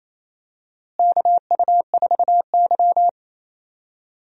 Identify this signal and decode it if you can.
Morse code, heard as a keyed tone, about 28 wpm, 700 Hz. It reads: KU4Y
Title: USA Call Signs - 4 Characters - Repeated 2x - 28wpm